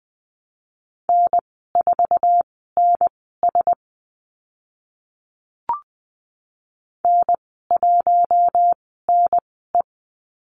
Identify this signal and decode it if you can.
Morse code, heard as a keyed tone, about 20 wpm, spaced as for 14 wpm, 700 Hz. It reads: N4NS N1NE
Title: 062 Lesson 11g - Letter h - Call Signs 20wpm